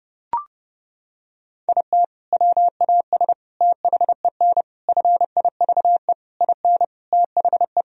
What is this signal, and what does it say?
Morse code, heard as a keyed tone, about 30 wpm, 700 Hz. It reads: IT WAS THEN FIVE IN THE
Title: Sentences from Top 1000 Words - Review Part 2 - 30wpm